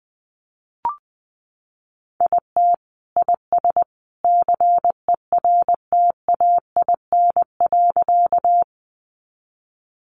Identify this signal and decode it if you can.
Morse code, heard as a keyed tone, about 20 wpm, 700 Hz. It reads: IT IS CERTAIN.
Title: Sentences from Top 500 Words 20wpm